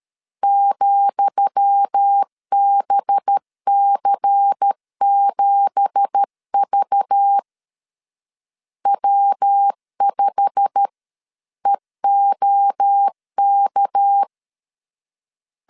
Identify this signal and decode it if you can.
Morse code, heard as a keyed tone, about 13 wpm, 800 Hz. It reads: ,BC7V W5EOK
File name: LEZIONI / 10 SIGNS / 10D COMA 12wpm 2544